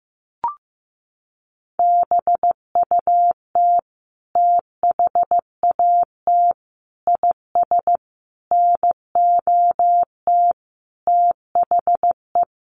Morse code, heard as a keyed tone, about 15 wpm, 700 Hz. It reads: BUT THAT IS NOT THE
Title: Sentences from Top 1500 Words 15wpm